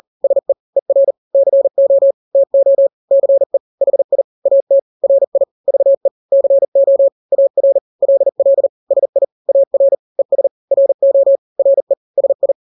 Morse code, heard as a keyed tone, about 40 wpm, 550 Hz. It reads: SE ER CO TO CE HI AT RI VE CO AR LL SI AR ES RO RE SI
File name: n-grams / 2-grams 40 35 WPM